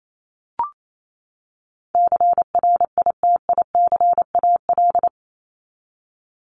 Morse code, heard as a keyed tone, about 28 wpm, 700 Hz. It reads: CRITICAL
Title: Top 1001-1250 Words 28wpm